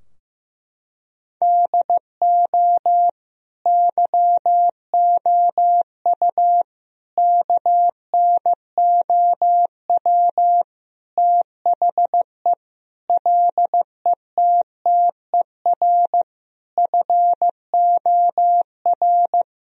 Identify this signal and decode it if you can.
Morse code, heard as a keyed tone, about 15 wpm, 700 Hz. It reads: DO YOU KNOW THE LETTER FOR
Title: Sentences from Top 1000 Words - Review Part 2 - 15wpm